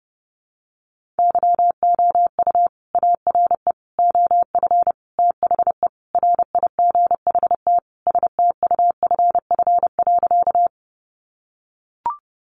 Morse code, heard as a keyed tone, about 30 wpm, 700 Hz. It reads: YOU ARE OF THE RIGHT STUFF.
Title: Sentences from Top 1400 Words 30wpm